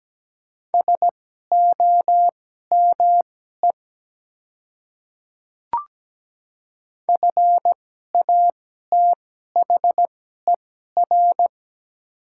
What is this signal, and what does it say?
Morse code, heard as a keyed tone, about 17 wpm, spaced as for 12 wpm, 700 Hz. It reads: SOME FATHER
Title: 2x Character Spacing - Top 200 Words - 17wpm